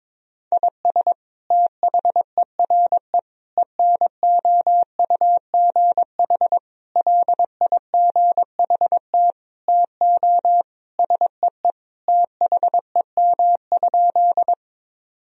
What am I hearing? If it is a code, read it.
Morse code, heard as a keyed tone, about 22 wpm, 700 Hz. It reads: IS THERE ENOUGH LIGHT TO SEE THEM?
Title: Sentences from Top 1000 Words - Review Part 1 - 22wpm